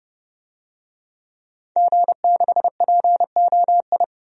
Morse code, heard as a keyed tone, about 31 wpm, 700 Hz. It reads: G6POI
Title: DXCC-100 Call Signs - 5 Characters - 30wpm